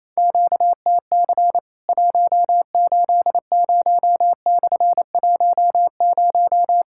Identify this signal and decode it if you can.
Morse code, heard as a keyed tone, about 27 wpm, 700 Hz. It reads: QTC 180/10
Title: Worked All Europe DX Contest - QTC Exchange - 28wpm